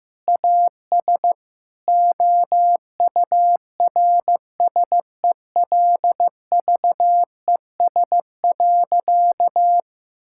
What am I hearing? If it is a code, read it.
Morse code, heard as a keyed tone, about 15 wpm, 700 Hz. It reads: AS OURSELVES.
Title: Sentences from Top 1600 Words 15wpm